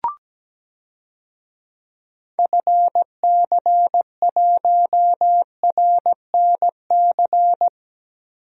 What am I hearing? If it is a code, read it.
Morse code, heard as a keyed tone, about 17 wpm, 700 Hz. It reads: FC1RNC
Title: DXCC-100 Call Signs - 6 Characters - Repeated 3x - 17wpm